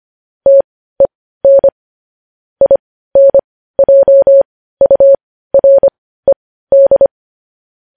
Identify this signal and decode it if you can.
Morse code, heard as a keyed tone, about 25 wpm, spaced as for 15 wpm, 550 Hz. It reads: TEN INJURED
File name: podcast / News headlines-1x-25x15